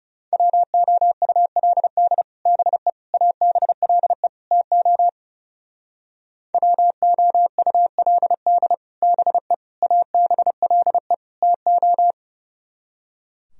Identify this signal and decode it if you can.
Morse code, heard as a keyed tone, about 37 wpm, 700 Hz. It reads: WOULD BE ABLE TO WOULD BE ABLE TO
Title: Speed-Racing - Sets of 4 Words 30wpm